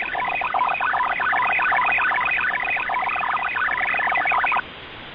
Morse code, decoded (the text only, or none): none